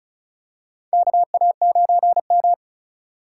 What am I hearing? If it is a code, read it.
Morse code, heard as a keyed tone, about 36 wpm, 700 Hz. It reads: KA9M
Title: USA Call Signs - 4 Characters - 2x Character Spacing - 35wpm